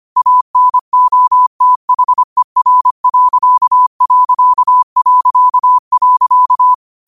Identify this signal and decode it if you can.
Morse code, heard as a keyed tone, about 25 wpm, 1000 Hz. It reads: ANOTHER....
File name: qotdfiles / 25wpm